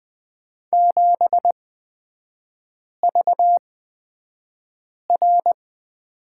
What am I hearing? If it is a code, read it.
Morse code, heard as a keyed tone, about 20 wpm, spaced as for 5 wpm, 700 Hz. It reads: 7VR
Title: 255 Lesson 41e - Period - Three Characters 20wpm